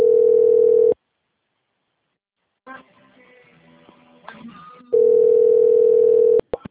Ringback tone (North America), 440 + 480 Hz.